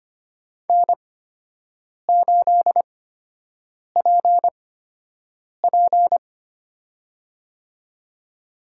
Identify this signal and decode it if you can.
Morse code, heard as a keyed tone, about 25 wpm, spaced as for 7 wpm, 700 Hz. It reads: N8PP
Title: USA Call Signs - 4 Characters - 8x Character Spacing - 25wpm